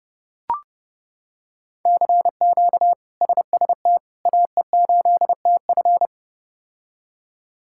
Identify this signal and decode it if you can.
Morse code, heard as a keyed tone, about 32 wpm, 700 Hz. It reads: CQ SST AE8TF
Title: K1USN SST QSO Practice - 30wpm